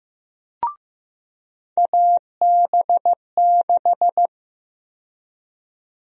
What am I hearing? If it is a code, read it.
Morse code, heard as a keyed tone, about 15 wpm, 700 Hz. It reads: AB6
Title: USA Call Sign Prefixes - 15wpm